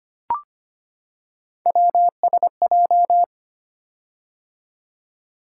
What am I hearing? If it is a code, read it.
Morse code, heard as a keyed tone, about 25 wpm, 700 Hz. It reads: WSJ
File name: Three Letter-Number Combinations - 25wpm